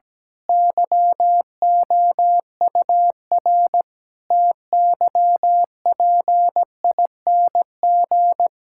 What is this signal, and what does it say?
Morse code, heard as a keyed tone, about 17 wpm, 700 Hz. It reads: YOUR TYPING
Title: Pangram Challenge v2 - 17wpm